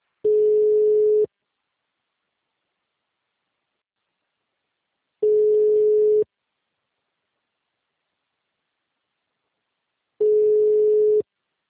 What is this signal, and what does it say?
Ringback tone (Europe, 425 Hz).